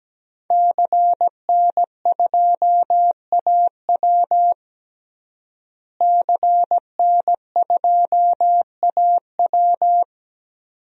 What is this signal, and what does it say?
Morse code, heard as a keyed tone, about 17 wpm, 700 Hz. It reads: CN2AW CN2AW
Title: DXCC-100 Call Signs - 5 Characters - Repeated 2x - 17wpm